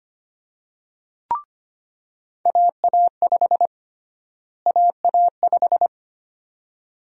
Morse code, heard as a keyed tone, about 25 wpm, 700 Hz. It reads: AA5 AA5
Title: USA Call Sign Prefixes - Repeated 2x - 25wpm